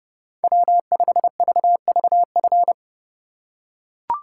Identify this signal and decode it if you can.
Morse code, heard as a keyed tone, about 27 wpm, 700 Hz. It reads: W5VVF